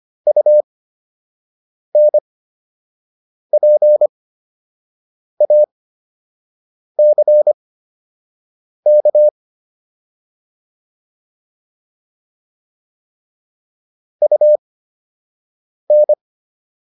Morse code, heard as a keyed tone, about 25 wpm, spaced as for 6 wpm, 600 Hz. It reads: UNPACK UN